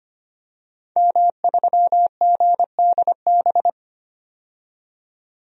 Morse code, heard as a keyed tone, about 25 wpm, 700 Hz. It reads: M3GDB
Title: DXCC-100 Call Signs - 5 Characters - 25wpm